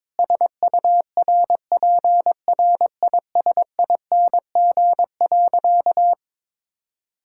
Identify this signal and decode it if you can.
Morse code, heard as a keyed tone, about 22 wpm, 700 Hz. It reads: SURPRISING.
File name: practice / 22wpm / Sentences with 4-Word Phrases - 22wpm